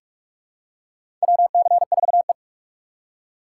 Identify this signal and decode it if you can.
Morse code, heard as a keyed tone, about 41 wpm, 700 Hz. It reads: WC4E
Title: USA Call Signs - 4 Characters - 6x Character Spacing - 45wpm